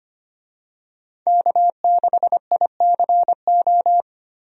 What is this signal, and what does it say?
Morse code, heard as a keyed tone, about 25 wpm, 700 Hz. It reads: K6ICO
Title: USA Call Signs - 5 Characters - 25wpm